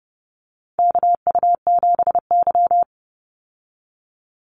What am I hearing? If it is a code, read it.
Morse code, heard as a keyed tone, about 30 wpm, 700 Hz. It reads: KU7Y
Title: Call Signs Repeated 2x - 30wpm